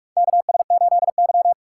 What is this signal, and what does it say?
Morse code, heard as a keyed tone, about 46 wpm, 700 Hz. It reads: KS8Y
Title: USA Call Signs - 4 Characters - 6x Character Spacing - 45wpm